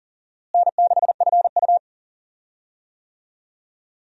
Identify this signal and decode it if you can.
Morse code, heard as a keyed tone, about 40 wpm, 700 Hz. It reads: N6FU
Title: USA Call Signs - 4 Characters - Repeated 3x - 40wpm